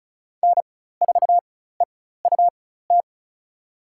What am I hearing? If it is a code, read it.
Morse code, heard as a keyed tone, about 34 wpm, spaced as for 16 wpm, 700 Hz. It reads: N4EUT
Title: USA Call Signs - 5 Characters - 4x Character Spacing - 35wpm